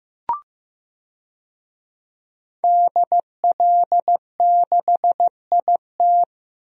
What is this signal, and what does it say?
Morse code, heard as a keyed tone, about 15 wpm, 700 Hz. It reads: DL6IT
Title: DXCC-100 Call Signs - 5 Characters - Repeated 2x - 15wpm